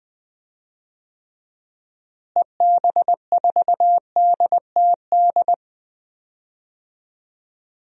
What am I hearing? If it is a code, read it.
Morse code, heard as a keyed tone, about 20 wpm, 700 Hz. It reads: EB4DTD
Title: DXCC-100 Call Signs - 6 Characters - Repeated 3x - 20wpm